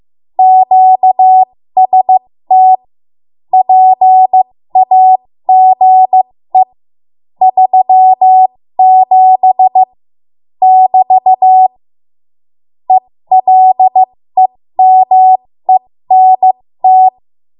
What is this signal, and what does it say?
Morse code, heard as a keyed tone, about 15 wpm, spaced as for 13 wpm, 750 Hz. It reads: QST PAGE 37 <BT> ELEMENT
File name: Morse / 13WPM